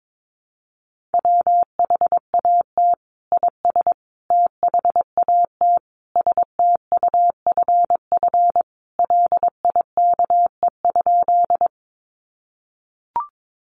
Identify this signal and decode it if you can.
Morse code, heard as a keyed tone, about 22 wpm, 700 Hz. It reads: WHAT IS THAT STUFF LIKE?
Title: Sentences from Top 1400 Words 22wpm